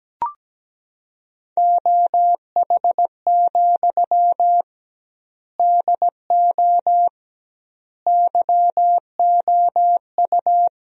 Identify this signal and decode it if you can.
Morse code, heard as a keyed tone, about 17 wpm, 700 Hz. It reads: OH, DO YOU
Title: 2x Word-Spacing - Sentences from Top 500 Words - 17wpm